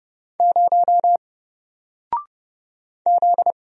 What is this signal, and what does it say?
Morse code, heard as a keyed tone, about 30 wpm, 700 Hz. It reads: 0 Z
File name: Single-Letter-Number-30wpm